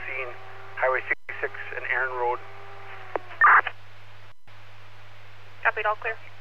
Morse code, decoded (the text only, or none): none